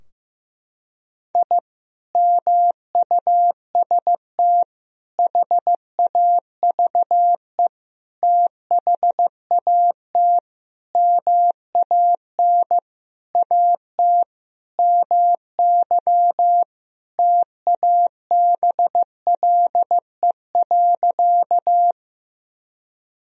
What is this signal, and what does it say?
Morse code, heard as a keyed tone, about 15 wpm, 700 Hz. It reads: I MUST HAVE THAT MAN AT MY TABLE.
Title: Sentences from Top 1000 Words - Review Part 2 - 15wpm